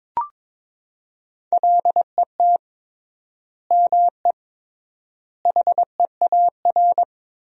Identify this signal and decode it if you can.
Morse code, heard as a keyed tone, about 22 wpm, 700 Hz. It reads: LET ME HEAR
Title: 3x Word-Spacing - Sentences from Top 400 Words - 22wpm